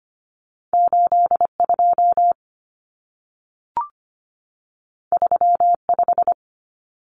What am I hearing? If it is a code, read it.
Morse code, heard as a keyed tone, about 25 wpm, 700 Hz. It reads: 82 35
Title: Common Cut Numbers - 2-Digit 25wpm